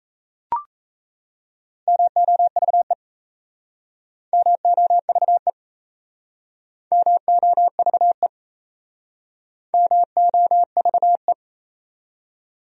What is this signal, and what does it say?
Morse code, heard as a keyed tone, about 37 wpm, 700 Hz. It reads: MOVE MOVE MOVE MOVE